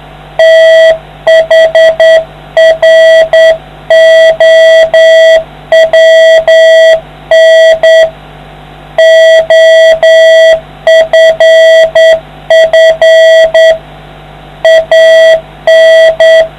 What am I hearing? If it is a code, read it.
Morse code, heard as a keyed tone, about 9 wpm, 650 Hz. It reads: THROWN OFF AN